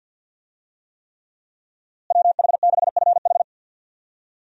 Morse code, heard as a keyed tone, about 52 wpm, 700 Hz. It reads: WH6FH
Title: USA Call Signs - 5 Characters - Repeated 3x - 50wpm